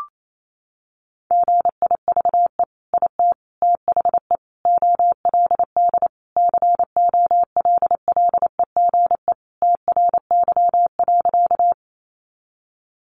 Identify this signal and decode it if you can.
Morse code, heard as a keyed tone, about 28 wpm, 700 Hz. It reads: GIVE IT THE OLD COLLEGE TRY.
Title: Sentences from Top 1300 Words 28wpm